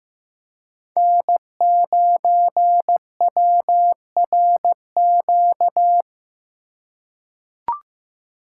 Morse code, heard as a keyed tone, about 15 wpm, 700 Hz. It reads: N9WRQ
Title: USA Call Signs - 5 Characters - 4x Character Spacing - 15wpm